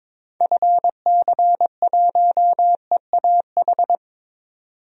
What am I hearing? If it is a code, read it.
Morse code, heard as a keyed tone, about 22 wpm, 700 Hz. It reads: FC1EAH
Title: DXCC-100 Call Signs - 6 Characters - 22wpm